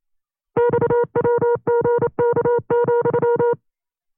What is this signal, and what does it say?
Morse code, heard as a keyed tone, about 29 wpm, 480 Hz. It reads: XWGK,
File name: CW-practice/25WPM